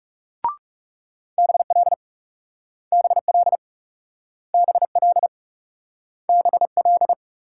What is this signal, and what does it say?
Morse code, heard as a keyed tone, about 41 wpm, 700 Hz. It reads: BL BL BL BL